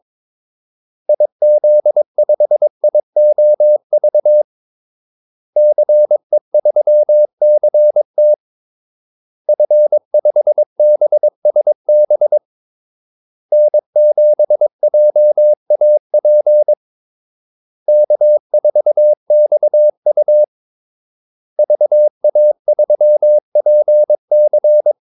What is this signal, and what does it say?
Morse code, heard as a keyed tone, about 22 wpm, 600 Hz. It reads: IZ5IOV CE3CT F5BSB N7JAP K4XU VA3PC
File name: Callsigns-22wpm